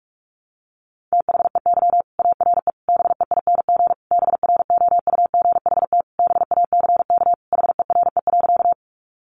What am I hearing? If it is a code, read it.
Morse code, heard as a keyed tone, about 44 wpm, 700 Hz. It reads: THEY ARE BEING BROUGHT BACK HERE.
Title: Sentences from Top 600 Words 45wpm